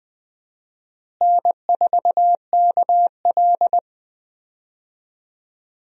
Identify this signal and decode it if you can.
Morse code, heard as a keyed tone, about 20 wpm, 700 Hz. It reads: N4KL
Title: USA Call Signs - 4 Characters - Repeated 2x - 20wpm